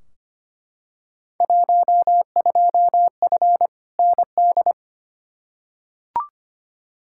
Morse code, heard as a keyed tone, about 25 wpm, 700 Hz. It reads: 12F ND
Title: Field Day Exchanges Encore - 25wpm